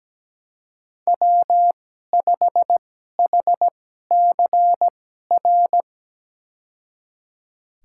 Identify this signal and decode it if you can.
Morse code, heard as a keyed tone, about 17 wpm, spaced as for 12 wpm, 700 Hz. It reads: W5HCR
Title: USA Call Signs - 2x Character Spacing - 17wpm